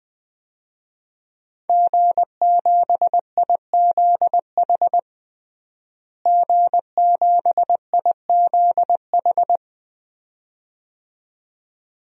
Morse code, heard as a keyed tone, about 20 wpm, 700 Hz. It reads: G7IZH G7IZH
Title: DXCC-100 Call Signs - 5 Characters - Repeated 2x - 20wpm